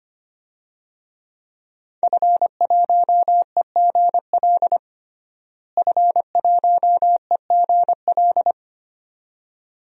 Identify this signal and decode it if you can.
Morse code, heard as a keyed tone, about 25 wpm, 700 Hz. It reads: F1EGL F1EGL
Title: DXCC-100 Call Signs - 5 Characters - Repeated 2x - 25wpm